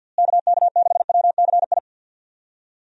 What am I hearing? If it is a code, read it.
Morse code, heard as a keyed tone, about 52 wpm, 700 Hz. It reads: KK6WCI
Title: USA Call Signs - 6 Characters - Repeated 2x - 50wpm